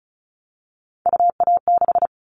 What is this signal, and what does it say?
Morse code, heard as a keyed tone, about 36 wpm, 700 Hz. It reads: UA6
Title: Callsign Prefixes for Top 25 DXCC - 35wpm